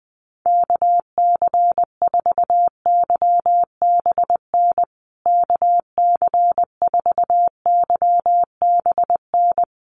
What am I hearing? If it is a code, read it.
Morse code, heard as a keyed tone, about 20 wpm, 700 Hz. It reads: KC4YBN KC4YBN
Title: USA Call Signs Repeated 3x 20wpm